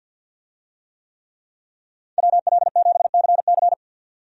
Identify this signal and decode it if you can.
Morse code, heard as a keyed tone, about 48 wpm, 700 Hz. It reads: WL7CC
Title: USA Call Signs - 5 Characters - 50wpm